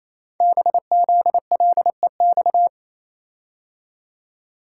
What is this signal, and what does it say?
Morse code, heard as a keyed tone, about 28 wpm, 700 Hz. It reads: BZLEX
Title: Code Groups - 5 Letters - 28wpm